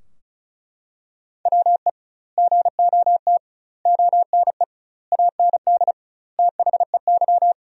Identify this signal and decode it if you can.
Morse code, heard as a keyed tone, about 34 wpm, 700 Hz. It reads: WE GOT ONE AND THEY
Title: 2x Word-Spacing - Sentences from Top 300 Words - 35wpm